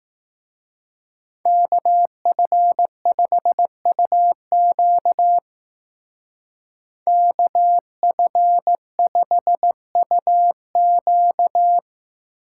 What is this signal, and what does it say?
Morse code, heard as a keyed tone, about 16 wpm, 700 Hz. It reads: KF5UQ KF5UQ
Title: Speed Racing - USA 2x2 Call Signs 15wpm